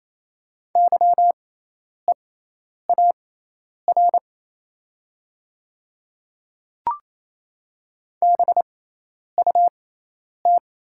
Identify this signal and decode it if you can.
Morse code, heard as a keyed tone, about 27 wpm, spaced as for 10 wpm, 700 Hz. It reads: YEAR BUT
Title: 6x Character Spacing - Top 200 Words - 28wpm